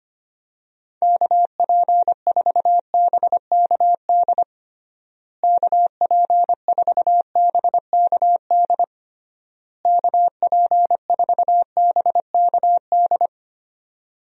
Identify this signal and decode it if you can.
Morse code, heard as a keyed tone, about 25 wpm, 700 Hz. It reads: KP4BKD KP4BKD KP4BKD
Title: DXCC-100 Call Signs - 6 Characters - Repeated 3x - 25wpm